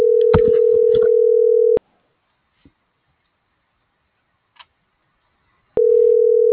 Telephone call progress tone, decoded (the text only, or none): ringback tone